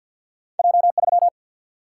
Morse code, heard as a keyed tone, about 51 wpm, 700 Hz. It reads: J 3
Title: Single-Letter Single-Number 50wpm